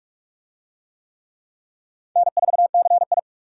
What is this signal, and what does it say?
Morse code, heard as a keyed tone, about 39 wpm, 700 Hz. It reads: N4CI